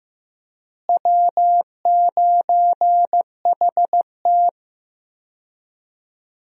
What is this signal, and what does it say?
Morse code, heard as a keyed tone, about 15 wpm, 700 Hz. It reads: W9HT
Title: USA Call Signs - 4 Characters - Repeated 3x - 15wpm